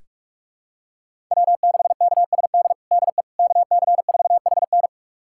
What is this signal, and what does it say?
Morse code, heard as a keyed tone, about 46 wpm, 700 Hz. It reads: W6KSD DE KC4HN
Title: USA Callsign DE Callsign - 5 Characters - 45wpm